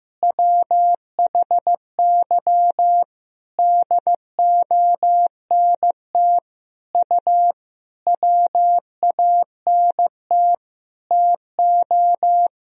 Morse code, heard as a keyed tone, about 15 wpm, 700 Hz. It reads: WHY DONT U WANT TO
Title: Sentences from Top 300 Words and Abbreviations 15wpm